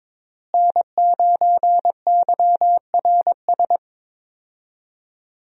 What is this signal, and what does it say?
Morse code, heard as a keyed tone, about 22 wpm, 700 Hz. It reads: N9YRS
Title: USA Call Signs - 5 Characters - Repeated 3x - 22wpm